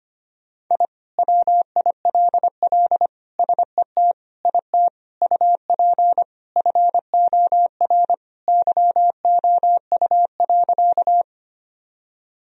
Morse code, heard as a keyed tone, about 25 wpm, 700 Hz. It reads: I WILL SET IT UP FOR YOU.
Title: Sentences from Top 1000 Words - Review Part 1 - 25wpm